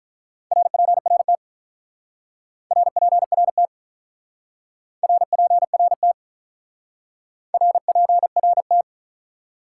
Morse code, heard as a keyed tone, about 44 wpm, 700 Hz. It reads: RPRT RPRT RPRT RPRT